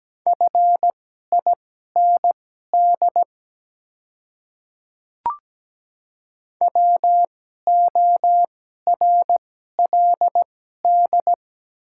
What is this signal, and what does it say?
Morse code, heard as a keyed tone, about 17 wpm, spaced as for 12 wpm, 700 Hz. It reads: FIND WORLD
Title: 2x Character Spacing - Top 300 Words - 17wpm